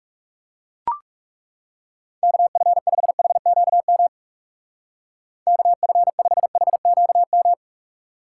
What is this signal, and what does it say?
Morse code, heard as a keyed tone, about 36 wpm, 700 Hz. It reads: KF5HQM KF5HQM
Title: Speed Racing - USA 2x3 Call Signs 30wpm